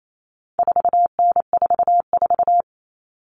Morse code, heard as a keyed tone, about 28 wpm, 700 Hz. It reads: 4N44
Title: All Cut Numbers - 4-Digit 28wpm